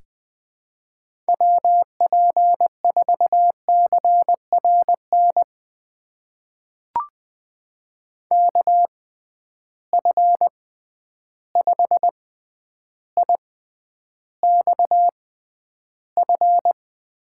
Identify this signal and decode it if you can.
Morse code, heard as a keyed tone, about 20 wpm, spaced as for 10 wpm, 700 Hz. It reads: WP4CRN KF5IXF
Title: USA Call Signs - 6 Characters - 6x Character Spacing - 20wpm